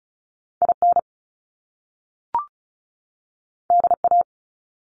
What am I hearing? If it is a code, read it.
Morse code, heard as a keyed tone, about 36 wpm, 700 Hz. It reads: IN DA
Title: Top Two Letter Combinations in Words 35wpm